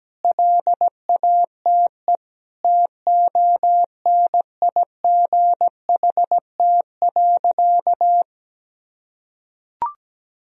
Morse code, heard as a keyed tone, about 17 wpm, 700 Hz. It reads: LATE TONIGHT.
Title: Sentences from Top 1000 Words - Review Part 4 - 17wpm